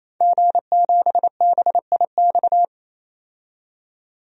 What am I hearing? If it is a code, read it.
Morse code, heard as a keyed tone, about 28 wpm, 700 Hz. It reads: G7BIX
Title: DXCC-100 Call Signs - 5 Characters - 28wpm